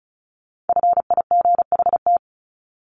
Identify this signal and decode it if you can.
Morse code, heard as a keyed tone, about 34 wpm, 700 Hz. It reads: FIGHT
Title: ING Suffix - Top 100 Words with Base Words 35wpm